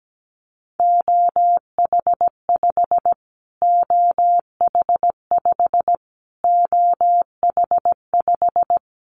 Morse code, heard as a keyed tone, about 17 wpm, 700 Hz. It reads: OH5 OH5 OH5
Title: Callsign Prefixes for Top 25 DXCC - Repeated 3x - 17wpm